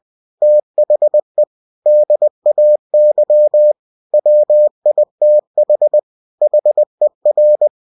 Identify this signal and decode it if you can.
Morse code, heard as a keyed tone, about 20 wpm, 600 Hz. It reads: THE DAY WITH HER
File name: cw-books / winnie-the-pooh / winnie-20wpm-page-0111